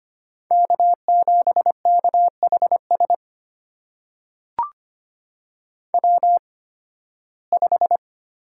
Morse code, heard as a keyed tone, about 25 wpm, spaced as for 16 wpm, 700 Hz. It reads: K7KHS W5
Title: USA Call Signs - 8x Character Spacing - 25wpm